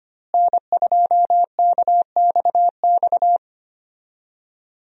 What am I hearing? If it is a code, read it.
Morse code, heard as a keyed tone, about 25 wpm, 700 Hz. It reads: N2KXX
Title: USA Call Signs - 5 Characters - Repeated 3x - 25wpm